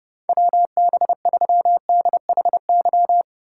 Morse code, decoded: WB3DHY

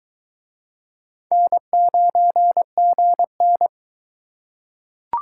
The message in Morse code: N9GN

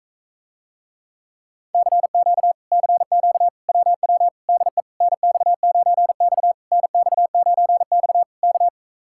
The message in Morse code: CQ CQ WW DE NX9X NX9X K

